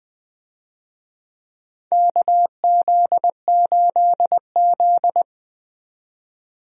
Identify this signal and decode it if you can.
Morse code, heard as a keyed tone, about 20 wpm, 700 Hz. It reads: KZ8Z
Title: USA Call Signs - 4 Characters - 20wpm